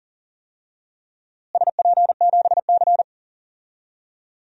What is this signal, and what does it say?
Morse code, heard as a keyed tone, about 40 wpm, 700 Hz. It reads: SP7C
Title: DXCC-100 Call Signs - 4 Characters - Repeated 3x - 40wpm